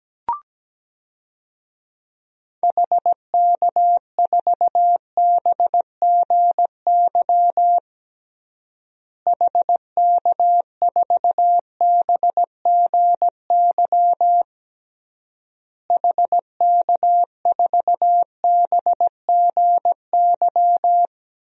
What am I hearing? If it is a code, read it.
Morse code, heard as a keyed tone, about 17 wpm, 700 Hz. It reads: HK4BGY HK4BGY HK4BGY
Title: DXCC-100 Call Signs - 6 Characters - Repeated 3x - 17wpm